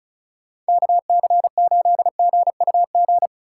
Morse code, heard as a keyed tone, about 36 wpm, 700 Hz. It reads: KC8GUG
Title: USA Call Signs - 6 Characters - 3x Character Spacing - 35wpm